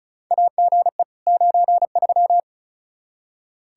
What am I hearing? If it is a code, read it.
Morse code, heard as a keyed tone, about 36 wpm, 700 Hz. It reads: AGE 93
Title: Practice QSO - Exchange 3 - More Info - 35wpm